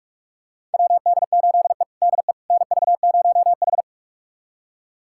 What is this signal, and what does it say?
Morse code, heard as a keyed tone, about 46 wpm, 700 Hz. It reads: WD8E DE NV0H
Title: USA Callsign DE Callsign - 4 Characters - 45wpm